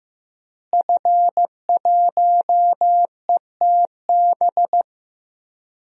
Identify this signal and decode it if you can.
Morse code, heard as a keyed tone, about 15 wpm, 700 Hz. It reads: F1ETB
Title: DXCC-100 Call Signs - 5 Characters - Repeated 3x - 15wpm